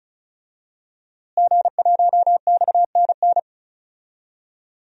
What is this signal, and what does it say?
Morse code, heard as a keyed tone, about 36 wpm, 700 Hz. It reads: G1XNN
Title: DXCC-100 Call Signs - 5 Characters - Repeated 2x - 35wpm